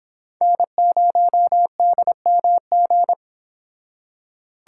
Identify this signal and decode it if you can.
Morse code, heard as a keyed tone, about 29 wpm, 700 Hz. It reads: N0DMG